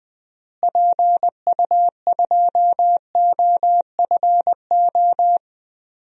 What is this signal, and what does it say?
Morse code, heard as a keyed tone, about 20 wpm, 700 Hz. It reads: PU2OFO
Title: DXCC-100 Call Signs - 6 Characters - Repeated 3x - 20wpm